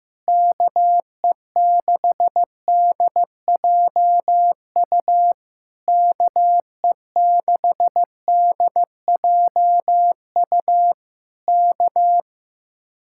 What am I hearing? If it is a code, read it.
Morse code, heard as a keyed tone, about 15 wpm, 700 Hz. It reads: KE6DJU KE6DJU K
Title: Speed-Racing Common QSO Phrases Encore - 15wpm